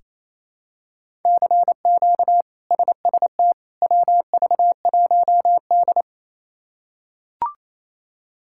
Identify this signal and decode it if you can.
Morse code, heard as a keyed tone, about 27 wpm, 700 Hz. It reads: CQ SST WV1D
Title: K1USN SST QSO Practice - 28wpm